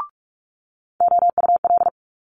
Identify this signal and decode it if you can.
Morse code, heard as a keyed tone, about 46 wpm, 700 Hz. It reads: OUL
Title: Multi-letter Phonemes Quick Fire 45wpm